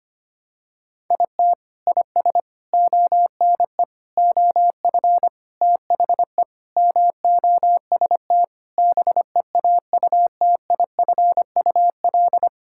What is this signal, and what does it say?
Morse code, heard as a keyed tone, about 25 wpm, 700 Hz. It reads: IT IS ONE OF THE MOST BEAUTIFUL
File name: practice / 25wpm / Sentences with 3-Word Phrases - 25wpm